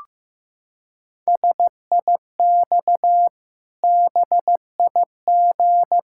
Morse code, heard as a keyed tone, about 15 wpm, 700 Hz. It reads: SIX BIG
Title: Pangram Challenge v2 - 15wpm